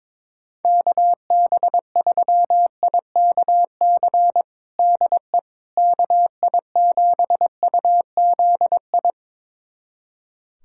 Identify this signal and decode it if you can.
Morse code, heard as a keyed tone, about 22 wpm, 700 Hz. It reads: KB3IKC DE KI7UZI
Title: USA Callsign DE Callsign - 6 Characters - 22wpm